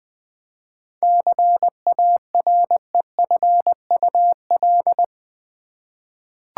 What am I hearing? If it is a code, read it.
Morse code, heard as a keyed tone, about 20 wpm, 700 Hz. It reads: CAREFUL
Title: 7 Letter Words - First 3 Letters Given - 20wpm